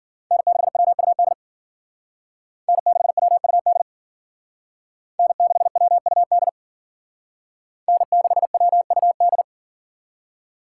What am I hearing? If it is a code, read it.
Morse code, heard as a keyed tone, about 52 wpm, 700 Hz. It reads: N6WUD N6WUD N6WUD N6WUD